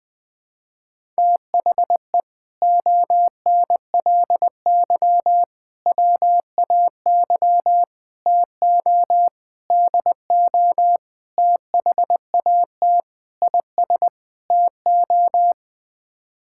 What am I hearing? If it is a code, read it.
Morse code, heard as a keyed tone, about 20 wpm, 700 Hz. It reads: THE ONLY WAY TO DO THAT IS TO
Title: Sets of 8 Words - 20wpm